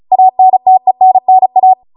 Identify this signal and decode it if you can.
Morse code, heard as a keyed tone, about 36 wpm, 750 Hz. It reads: ANTENNA